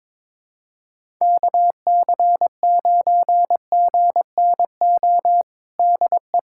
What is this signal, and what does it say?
Morse code, heard as a keyed tone, about 22 wpm, 700 Hz. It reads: KC9GNO DE
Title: USA Callsign DE Callsign - 6 Characters - 22wpm